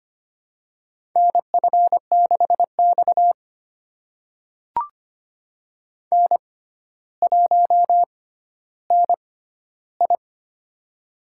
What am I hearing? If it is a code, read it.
Morse code, heard as a keyed tone, about 25 wpm, spaced as for 13 wpm, 700 Hz. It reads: NF6X N1NI
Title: USA Call Signs - 4 Characters - 6x Character Spacing - 25wpm